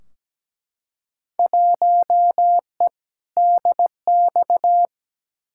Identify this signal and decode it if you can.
Morse code, heard as a keyed tone, about 17 wpm, 700 Hz. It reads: 1E DX
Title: Field Day Exchanges Encore - 17wpm